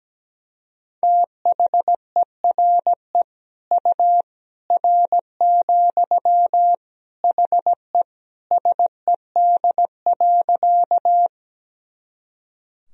Morse code, heard as a keyed tone, about 17 wpm, 700 Hz. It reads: THERE U R, HE SED.
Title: Sentences from Top 100 Words and Abbreviations 17wpm